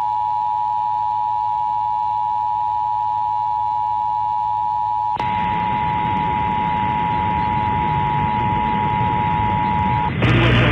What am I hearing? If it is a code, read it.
Emergency Alert System (SAME) tones (the 853 and 960 Hz attention tone), with noise. It is heard from 0.7 to 9.9 s.